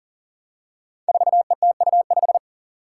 Morse code, heard as a keyed tone, about 40 wpm, 700 Hz. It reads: 4ETU5
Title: Five Challenging Letter-Number Combinations 40wpm